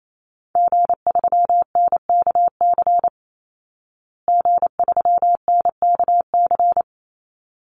Morse code, heard as a keyed tone, about 28 wpm, 700 Hz. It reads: G3NKC G3NKC
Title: Call Signs Repeated 3x - 28wpm